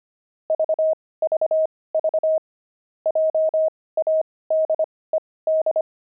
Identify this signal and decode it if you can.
Morse code, heard as a keyed tone, about 25 wpm, spaced as for 18 wpm, 630 Hz. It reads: VVV JADED
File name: pangram-fs-18-cs-25